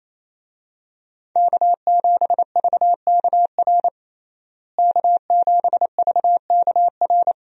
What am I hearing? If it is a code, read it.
Morse code, heard as a keyed tone, about 28 wpm, 700 Hz. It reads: K7VKR K7VKR
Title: USA Call Signs - 5 Characters - Repeated 2x - 28wpm